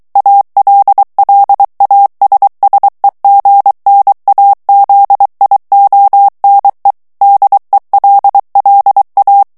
Morse code, heard as a keyed tone, about 23 wpm, 800 Hz. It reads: ALLASSEGNAZIONE DELLA
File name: ITALIEN texte clair 2 vitesse 22 mots minute